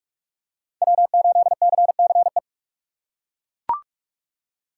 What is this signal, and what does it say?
Morse code, heard as a keyed tone, about 42 wpm, 700 Hz. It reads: W8CCE